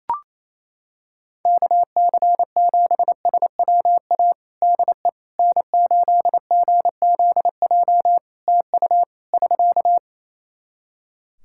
Morse code, heard as a keyed tone, about 28 wpm, 700 Hz. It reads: KC7SWA DE N8GZJ TU <SK>
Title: Practice QSO - Exchange 4 - Ending - 28wpm